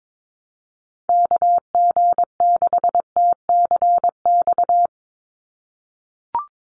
Morse code, heard as a keyed tone, about 22 wpm, 700 Hz. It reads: KG6TCX
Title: USA Call Signs 22wpm